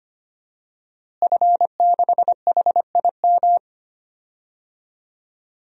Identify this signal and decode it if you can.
Morse code, heard as a keyed tone, about 25 wpm, 700 Hz. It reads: F6HIM
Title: DXCC-100 Call Signs - 5 Characters - Repeated 2x - 25wpm